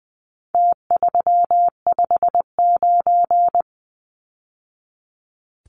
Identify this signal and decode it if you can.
Morse code, heard as a keyed tone, about 20 wpm, 700 Hz. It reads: T359
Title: Numbers - Contest Style - 4-Digit 20wpm